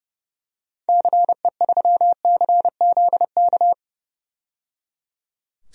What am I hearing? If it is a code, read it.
Morse code, heard as a keyed tone, about 30 wpm, 700 Hz. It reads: CE3CZK
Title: DXCC-100 Call Signs - 6 Characters - 30wpm